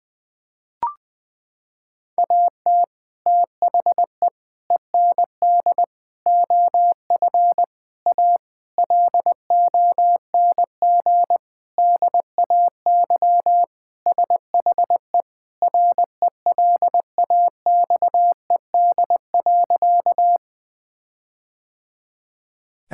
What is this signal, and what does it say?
Morse code, heard as a keyed tone, about 21 wpm, 700 Hz. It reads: AT THE END OF A LONG DAY SHE RELAXED.
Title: Sentences with 4-Word Phrases - 20wpm